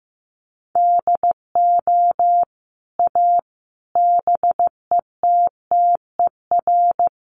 Morse code, heard as a keyed tone, about 15 wpm, 700 Hz. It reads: DO A BETTER
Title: Sets of 5 Words Encore2 15wpm